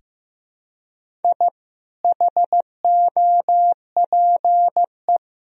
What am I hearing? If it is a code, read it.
Morse code, heard as a keyed tone, about 15 wpm, 700 Hz. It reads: I HOPE
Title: Sentences from Top 1000 Words - Review Part 1 - 15wpm